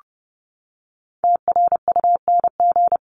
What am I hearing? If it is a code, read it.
Morse code, heard as a keyed tone, about 30 wpm, 700 Hz. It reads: TRUNG
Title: CWT Exchange 30wpm